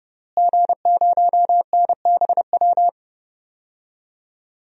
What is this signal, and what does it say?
Morse code, heard as a keyed tone, about 30 wpm, 700 Hz. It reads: G0NBW
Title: DXCC-100 Call Signs - 5 Characters - Repeated 3x - 30wpm